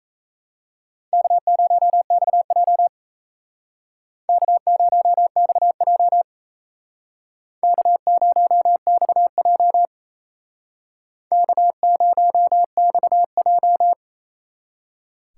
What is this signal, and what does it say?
Morse code, heard as a keyed tone, about 34 wpm, 700 Hz. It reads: K0XJ K0XJ K0XJ K0XJ
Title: Speed Racing - USA 1x2 Call Signs 28wpm